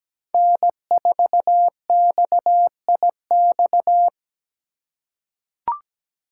Morse code, heard as a keyed tone, about 17 wpm, 700 Hz. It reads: N4XIX
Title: USA Call Signs - 2x Character Spacing - 17wpm